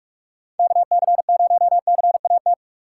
Morse code, heard as a keyed tone, about 44 wpm, 700 Hz. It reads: KC0CAT